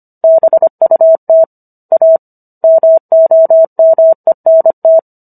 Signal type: Morse code